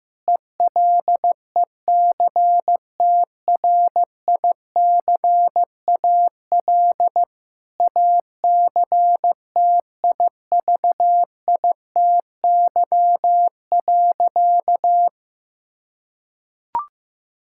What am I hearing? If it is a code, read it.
Morse code, heard as a keyed tone, about 15 wpm, 700 Hz. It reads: ELECTRICAL ACTIVITY.
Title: Sentences from Top 1000 Words - Review Part 6 - 15wpm